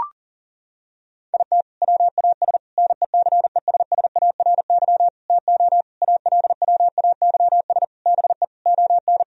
Morse code, heard as a keyed tone, about 40 wpm, 700 Hz. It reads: IT WAS NECESSARY TO ALWAYS BE ON
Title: Sentences from Top 1000 Words - Review Part 4 - 40wpm